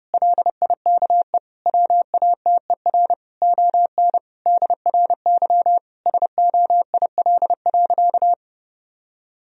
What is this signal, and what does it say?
Morse code, heard as a keyed tone, about 30 wpm, 700 Hz. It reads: LIKE WATER ON DRY SOIL.
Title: Sentences from Top 1000 Words - Review Part 4 - 30wpm